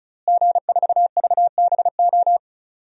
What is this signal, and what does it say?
Morse code, heard as a keyed tone, about 36 wpm, 700 Hz. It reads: G4VBO
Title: DXCC-100 Call Signs - 5 Characters - Repeated 3x - 35wpm